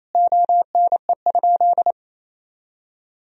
Morse code, heard as a keyed tone, about 28 wpm, 700 Hz. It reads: ONE?
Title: Sentences from Top 100 Words and Abbreviations 28wpm